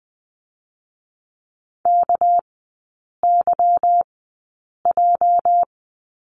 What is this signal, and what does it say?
Morse code, heard as a keyed tone, about 20 wpm, spaced as for 8 wpm, 700 Hz. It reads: KYJ
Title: 205 Lesson 33e - Letter j - Three Characters 20wpm